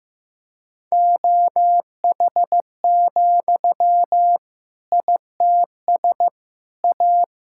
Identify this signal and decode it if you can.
Morse code, heard as a keyed tone, about 15 wpm, 700 Hz. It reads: OH, ITS A
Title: Sentences from Top 1000 Words - Review Part 5 - 15wpm